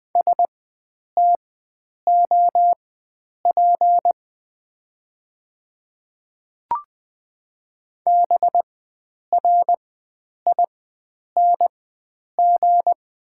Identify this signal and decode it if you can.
Morse code, heard as a keyed tone, about 20 wpm, spaced as for 9 wpm, 690 Hz. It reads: STOP BRING
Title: 4x Character Spacing - Top 500 Words - 20wpm